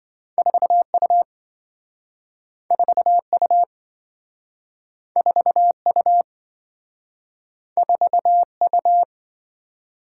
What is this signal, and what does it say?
Morse code, heard as a keyed tone, about 28 wpm, 700 Hz. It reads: 4U 4U 4U 4U